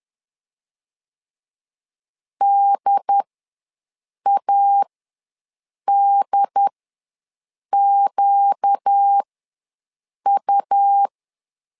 Morse code, heard as a keyed tone, about 10 wpm, spaced as for 6 wpm, 800 Hz. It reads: DADQU
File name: LEZIONI / 4 Q ws 10wpm 2306